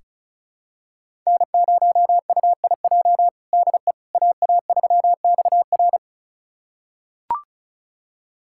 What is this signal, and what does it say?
Morse code, heard as a keyed tone, about 36 wpm, 700 Hz. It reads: N0UIJ DE AA3XR
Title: USA Callsign DE Callsign - 5 Characters - 35wpm